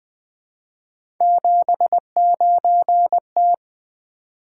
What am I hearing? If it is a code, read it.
Morse code, heard as a keyed tone, about 20 wpm, 700 Hz. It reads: 79T
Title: Three Letter-Number Combinations - 20wpm